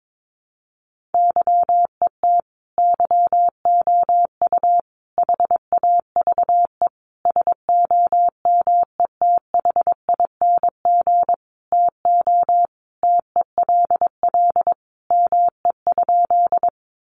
Morse code, heard as a keyed tone, about 22 wpm, 700 Hz. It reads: YET YOU HAVE SOMETHING TO TELL ME?
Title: Sentences from Top 600 Words 22wpm